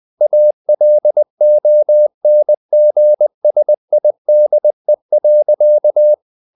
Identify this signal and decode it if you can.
Morse code, heard as a keyed tone, about 20 wpm, 600 Hz. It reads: ALONGSIDE.
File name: the old man and the sea / the old man and the sea part 1 20wpm